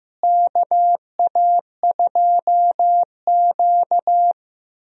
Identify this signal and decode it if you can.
Morse code, heard as a keyed tone, about 15 wpm, 700 Hz. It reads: KA2Q
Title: USA Call Signs - 4 Characters - Repeated 2x - 15wpm